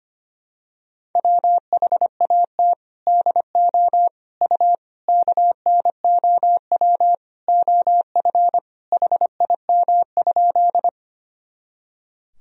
Morse code, heard as a keyed tone, about 25 wpm, 700 Hz. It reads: WHAT DO U KNOW OF HIM?
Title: Sentences from Top 100 Words and Abbreviations 25wpm